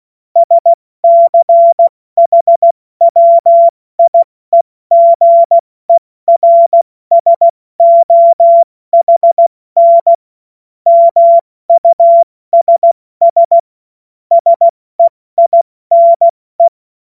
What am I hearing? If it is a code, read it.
Morse code, heard as a keyed tone, about 16 wpm, 680 Hz. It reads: SCHWIEGERSOHN MUSS SEINE